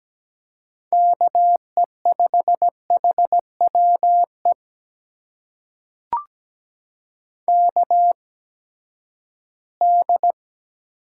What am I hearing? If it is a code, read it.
Morse code, heard as a keyed tone, about 17 wpm, spaced as for 12 wpm, 700 Hz. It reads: KE5HWE KD
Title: USA Call Signs - 6 Characters - 8x Character Spacing - 17wpm